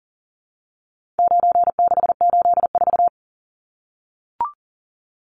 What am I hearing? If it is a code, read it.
Morse code, heard as a keyed tone, about 40 wpm, 700 Hz. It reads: 9684
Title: Numbers - 4-Digit 40wpm